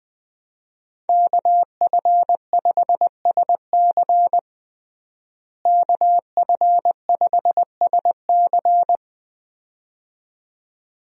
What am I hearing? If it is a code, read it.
Morse code, heard as a keyed tone, about 20 wpm, 700 Hz. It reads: KF5SC KF5SC
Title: USA Call Signs - 5 Characters - Repeated 2x - 20wpm